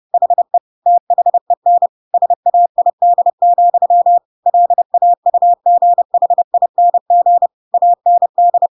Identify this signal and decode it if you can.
Morse code, heard as a keyed tone, about 30 wpm, 700 Hz. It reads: HE THEN SAID, LAUGHING AND
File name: Pinocchio / Pinocchio ch1 30 30 ik1hge